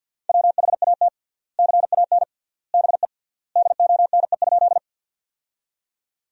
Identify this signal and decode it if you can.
Morse code, heard as a keyed tone, about 52 wpm, 700 Hz. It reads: WHAT CAN BE DONE?
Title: 3x Word-Spacing - Sentences from Top 500 Words - 50wpm